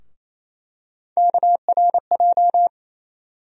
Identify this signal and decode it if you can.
Morse code, heard as a keyed tone, about 28 wpm, 700 Hz. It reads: KRJ
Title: USA Call Sign Postfixes - 3 Letters 28wpm